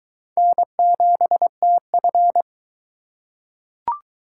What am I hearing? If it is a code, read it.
Morse code, heard as a keyed tone, about 22 wpm, 700 Hz. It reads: N7TF